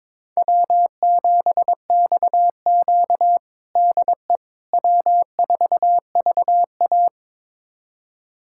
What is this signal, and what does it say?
Morse code, heard as a keyed tone, about 22 wpm, 700 Hz. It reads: W7XQ DE W4VA
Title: USA Callsign DE Callsign - 4 Characters - 22wpm